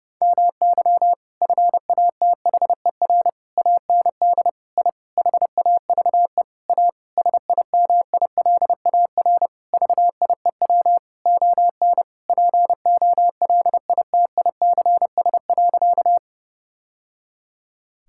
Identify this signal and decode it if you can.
Morse code, heard as a keyed tone, about 30 wpm, 700 Hz. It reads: MY FATHER AND I HAVE A SIMILAR VIEW ON POLITICS.
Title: Sentences from Top 1000 Words - Review Part 6 - 30wpm